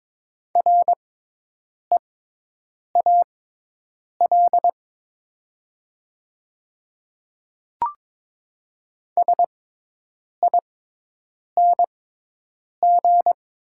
Morse code, heard as a keyed tone, about 22 wpm, spaced as for 8 wpm, 700 Hz. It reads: REAL SING
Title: 6x Character Spacing - Top 500 Words - 22wpm